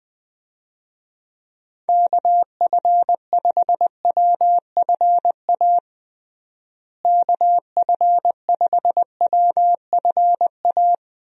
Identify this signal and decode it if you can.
Morse code, heard as a keyed tone, about 20 wpm, 700 Hz. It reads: KF5WFA KF5WFA
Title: USA Call Signs - 6 Characters - Repeated 2x - 20wpm